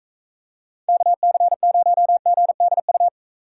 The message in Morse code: KC0GDU

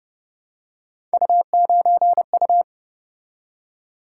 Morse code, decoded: U9U

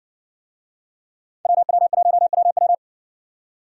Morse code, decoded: WW1WW

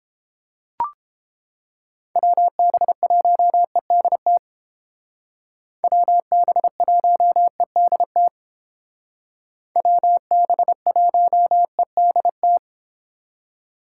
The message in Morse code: WB1EDT WB1EDT WB1EDT

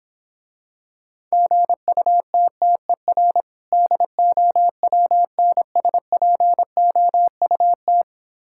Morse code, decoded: GUTTER DOWNSPOUT

